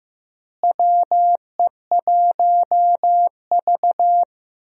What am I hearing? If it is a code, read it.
Morse code, heard as a keyed tone, about 15 wpm, 700 Hz. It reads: WE1V
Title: USA Call Signs - 4 Characters - Repeated 3x - 15wpm